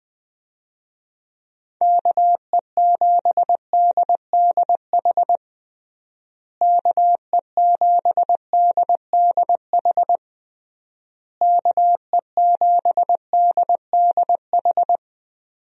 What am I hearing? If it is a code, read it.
Morse code, heard as a keyed tone, about 21 wpm, 700 Hz. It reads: KE7DDH KE7DDH KE7DDH
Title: USA Call Signs - 6 Characters - Repeated 3x - 20wpm